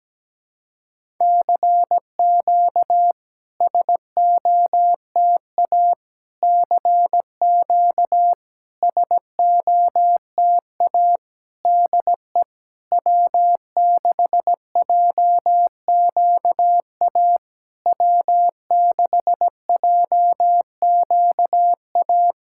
Morse code, heard as a keyed tone, about 17 wpm, 700 Hz. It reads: CQ SOTA CQ SOTA DE W6JQA W6JQA
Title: SOTA QSO Practice 17wpm